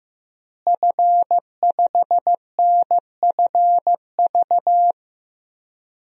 Morse code, heard as a keyed tone, about 15 wpm, 700 Hz. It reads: F5NFV
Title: DXCC-100 Call Signs - 5 Characters - Repeated 3x - 15wpm